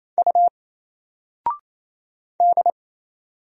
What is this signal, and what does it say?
Morse code, heard as a keyed tone, about 27 wpm, 700 Hz. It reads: U D
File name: Single-Letter-Number-28wpm